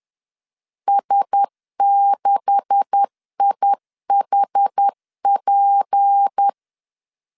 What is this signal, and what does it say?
Morse code, heard as a keyed tone, about 10 wpm, 800 Hz. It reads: S6IHP
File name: CW / DET / 1 9 1500C S A 10wpm 4511